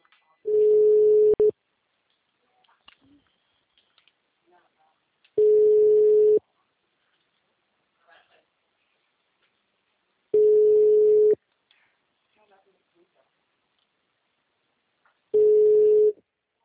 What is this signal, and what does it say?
Ringback tone (Europe, 425 Hz), with speech.